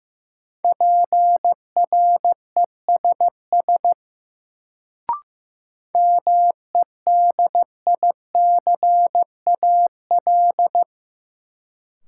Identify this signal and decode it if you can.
Morse code, heard as a keyed tone, about 15 wpm, 700 Hz. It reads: PRESS MEDICAL
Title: Sentences from Top 800 Words - New Word Drill - 15wpm